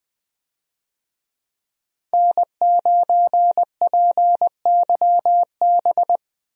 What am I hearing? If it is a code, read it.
Morse code, heard as a keyed tone, about 20 wpm, 700 Hz. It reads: N9PYB
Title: USA Call Signs - 5 Characters - Repeated 3x - 20wpm